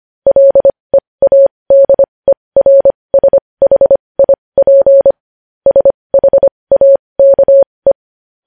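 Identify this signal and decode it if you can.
Morse code, heard as a keyed tone, about 25 wpm, spaced as for 20 wpm, 550 Hz. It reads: LEADERSHIP SHAKE
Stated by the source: Morse code news headlines sent at 20.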